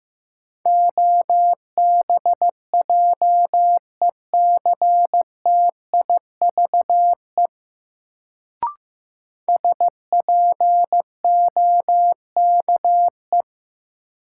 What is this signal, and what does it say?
Morse code, heard as a keyed tone, about 15 wpm, 700 Hz. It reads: OBJECTIVE SPOKE
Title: Sentences from Top 1200 Words - New Word Drill - 15wpm